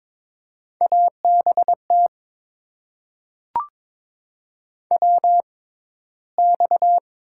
Morse code, heard as a keyed tone, about 23 wpm, spaced as for 14 wpm, 700 Hz. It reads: ABT WX
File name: Top 90 QSO Elements - 6x Character Spacing - 22wpm